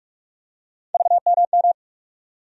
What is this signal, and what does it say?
Morse code, heard as a keyed tone, about 44 wpm, 700 Hz. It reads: VMM
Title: USA Call Sign Postfixes - 3 Letters 45wpm